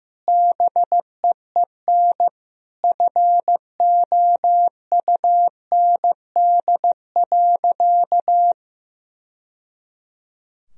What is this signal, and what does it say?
Morse code, heard as a keyed tone, about 15 wpm, 700 Hz. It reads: BEEN FOUND.
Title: Sentences from Top 1000 Words - Review Part 4 - 15wpm